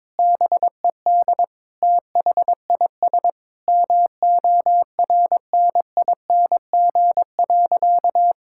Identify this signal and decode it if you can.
Morse code, heard as a keyed tone, about 22 wpm, 700 Hz. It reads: BED THIS MORNING.
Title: Sentences from Top 1000 Words - Review Part 6 - 22wpm